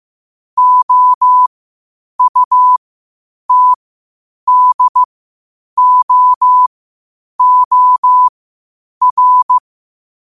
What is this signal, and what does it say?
Morse code, heard as a keyed tone, about 15 wpm, spaced as for 8 wpm, 1000 Hz. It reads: OUTDOOR
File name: qotdfiles / 8wpm